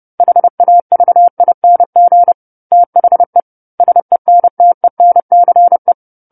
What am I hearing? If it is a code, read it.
Morse code, heard as a keyed tone, about 30 wpm, 700 Hz. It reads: HAVING THE SENTENCE
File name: 10-Chapter-10 30 WPM